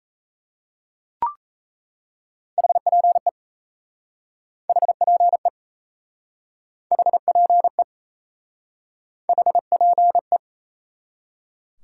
Morse code, heard as a keyed tone, about 38 wpm, 700 Hz. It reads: HPE HPE HPE HPE